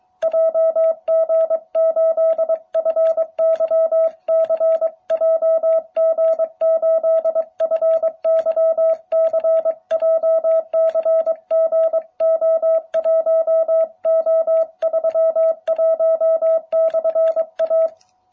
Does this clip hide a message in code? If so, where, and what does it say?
Yes: Morse code from 0.2 to 17.9 s, heard as a keyed tone, about 22 wpm, 650 Hz. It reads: JG8FYC JG8FYC JCGO1O31/A